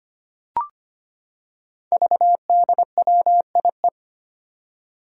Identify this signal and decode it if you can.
Morse code, heard as a keyed tone, about 25 wpm, 700 Hz. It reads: VDWIE
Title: Five Challenging Letter-Number Combinations 25wpm